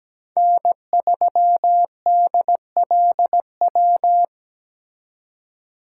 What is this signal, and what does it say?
Morse code, heard as a keyed tone, about 20 wpm, 700 Hz. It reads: N3DLW
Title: Speed Racing - USA 1x3 Call Signs 17wpm